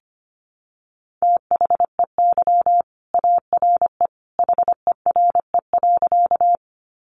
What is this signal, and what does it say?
Morse code, heard as a keyed tone, about 25 wpm, 700 Hz. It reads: THEY ARE HERE.
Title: Sentences from Top 200 Words 25wpm